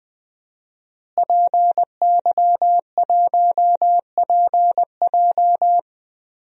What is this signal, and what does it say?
Morse code, heard as a keyed tone, about 20 wpm, 700 Hz. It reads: PY1PJ
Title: DXCC-100 Call Signs - 5 Characters - Repeated 3x - 20wpm